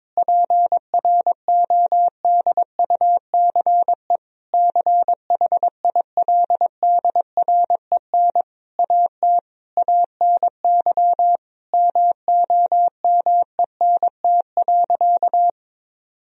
Morse code, heard as a keyed tone, about 22 wpm, 700 Hz. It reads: PRODUCE CHILDREN AT ANY MOMENT.
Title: Sentences from Top 1000 Words - Review Part 3 - 22wpm